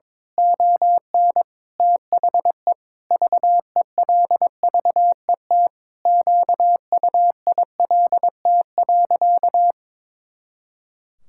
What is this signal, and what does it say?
Morse code, heard as a keyed tone, about 22 wpm, 700 Hz. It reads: ON THE VELVET QUILT.
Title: Pangram Challenge v2 - 22wpm